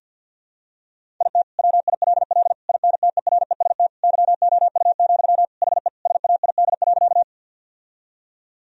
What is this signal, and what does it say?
Morse code, heard as a keyed tone, about 53 wpm, 700 Hz. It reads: IT WILL INTEREST YOU, HE SAID.
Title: Sentences from Top 1000 Words - Review Part 2 - 50wpm